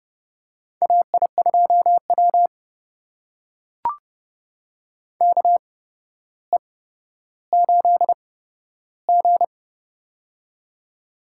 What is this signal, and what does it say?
Morse code, heard as a keyed tone, about 29 wpm, spaced as for 13 wpm, 700 Hz. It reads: AI2W KE8G
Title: USA Call Signs - 4 Characters - 8x Character Spacing - 30wpm